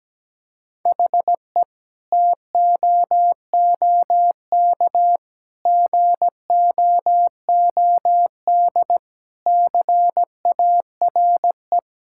Morse code, heard as a keyed tone, about 17 wpm, 700 Hz. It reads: HE TOOK GOOD CARE
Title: Sentences from Top 1000 Words - Review Part 5 - 17wpm